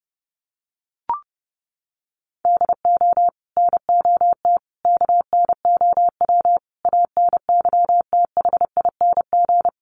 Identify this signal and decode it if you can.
Morse code, heard as a keyed tone, about 30 wpm, 700 Hz. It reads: DO NOT KNOW ANYTHING
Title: Sets of 5 Words 30wpm